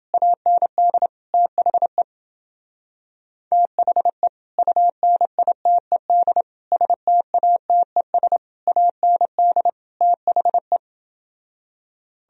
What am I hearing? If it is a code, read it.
Morse code, heard as a keyed tone, about 30 wpm, 700 Hz. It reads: AND THE THE UNITED STATES AND THE